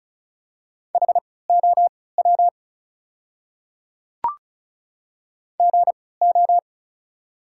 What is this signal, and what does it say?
Morse code, heard as a keyed tone, about 35 wpm, spaced as for 20 wpm, 690 Hz. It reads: HOW GO
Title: 3x Character Spacing - Top 100 Words - 35wpm